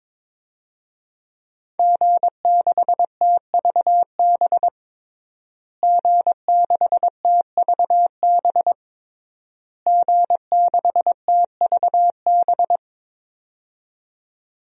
Morse code, heard as a keyed tone, about 22 wpm, 700 Hz. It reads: G6TVB G6TVB G6TVB
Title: DXCC-100 Call Signs - 5 Characters - Repeated 3x - 22wpm